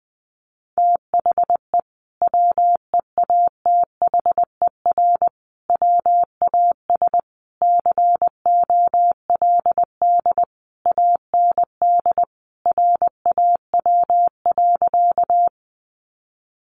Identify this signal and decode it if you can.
Morse code, heard as a keyed tone, about 20 wpm, 700 Hz. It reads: THE WEATHER WAS COLD AND RAW.
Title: Sentences from Top 1500 Words 20wpm